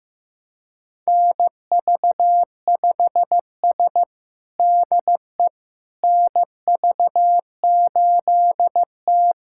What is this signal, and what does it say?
Morse code, heard as a keyed tone, about 16 wpm, 700 Hz. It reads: NV5S DE NV8T
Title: USA Callsign DE Callsign - 4 Characters - 15wpm